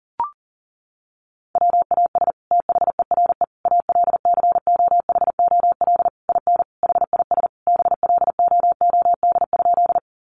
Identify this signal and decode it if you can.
Morse code, heard as a keyed tone, about 37 wpm, 700 Hz. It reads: WAS THERE ALCOHOL IN HIS BLOOD?
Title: Sentences from Top 1400 Words 40wpm